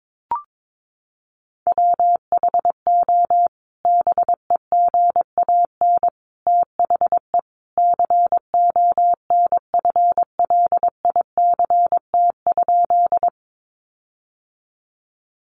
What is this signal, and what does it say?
Morse code, heard as a keyed tone, about 22 wpm, 700 Hz. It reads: WHO BEGAN THE CONFLICT?
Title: Sentences from Top 1300 Words 22wpm